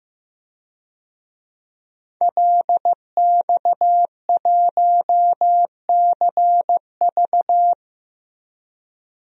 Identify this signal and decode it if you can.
Morse code, heard as a keyed tone, about 15 wpm, 700 Hz. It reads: LX1CV
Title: DXCC-100 Call Signs - 5 Characters - 15wpm